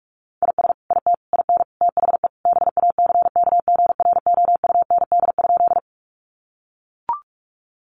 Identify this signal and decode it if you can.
Morse code, heard as a keyed tone, about 44 wpm, 700 Hz. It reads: IS IT IN THE BACKGROUND?
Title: Sentences from Top 1300 Words 45wpm